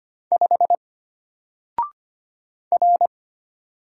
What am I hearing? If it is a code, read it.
Morse code, heard as a keyed tone, about 25 wpm, 700 Hz. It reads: 5 R
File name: Single-Letter-Number-25wpm